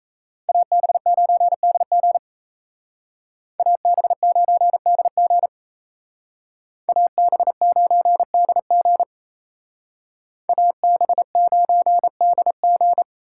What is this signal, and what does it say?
Morse code, heard as a keyed tone, about 32 wpm, 700 Hz. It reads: AB9DG AB9DG AB9DG AB9DG
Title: Speed Racing - USA 2x2 Call Signs 28wpm